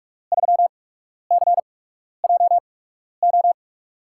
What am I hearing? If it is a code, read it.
Morse code, heard as a keyed tone, about 47 wpm, spaced as for 12 wpm, 700 Hz. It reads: 3CJO
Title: USA Call Signs - 6 Characters - 8x Character Spacing - 45wpm